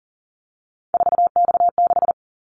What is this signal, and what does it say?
Morse code, heard as a keyed tone, about 40 wpm, 700 Hz. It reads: 4X6
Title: Callsign Prefixes for Top 50 DXCC - 40wpm